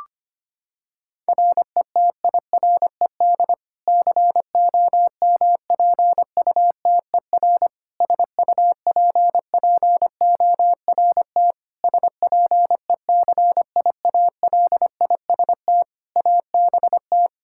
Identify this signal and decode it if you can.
Morse code, heard as a keyed tone, about 25 wpm, 700 Hz. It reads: RETIRED COMPUTER SUPPORT SPECIALIST ABT
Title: Practice QSO - Exchange 3 - More Info - 25wpm